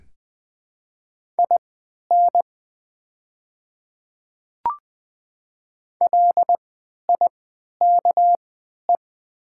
Morse code, heard as a keyed tone, about 20 wpm, spaced as for 11 wpm, 700 Hz. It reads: IN LIKE
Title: 3x Character Spacing - Top 100 Words - 20wpm